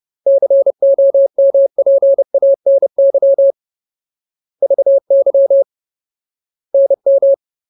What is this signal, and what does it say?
Morse code, heard as a keyed tone, about 30 wpm, 550 Hz. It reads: COMPANY VY NM
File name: Ultimate Word List / 30WPM Ultimate